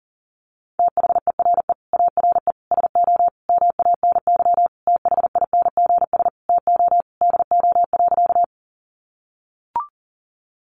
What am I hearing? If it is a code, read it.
Morse code, heard as a keyed tone, about 40 wpm, 700 Hz. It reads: THERE ARE SO MANY THINGS TO DO.
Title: Additional Easy Phrases 40wpm